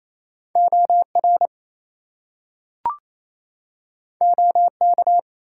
Morse code, heard as a keyed tone, about 29 wpm, 700 Hz. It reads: OR OK
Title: Two Letter Words - 28wpm